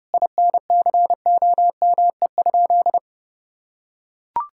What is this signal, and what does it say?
Morse code, heard as a keyed tone, about 30 wpm, 700 Hz. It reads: INCOME?
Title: Sentences from Top 1000 Words - Review Part 5 - 30wpm